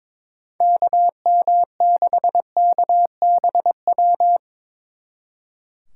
Morse code, heard as a keyed tone, about 24 wpm, 700 Hz. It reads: KM6KBW
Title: Speed Racing - USA Call Signs 22wpm